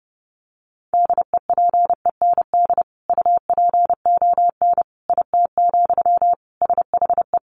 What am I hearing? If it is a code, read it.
Morse code, heard as a keyed tone, about 30 wpm, 700 Hz. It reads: DEPEND UPON IT, SHE
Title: Sentences from Top 1300 Words 30wpm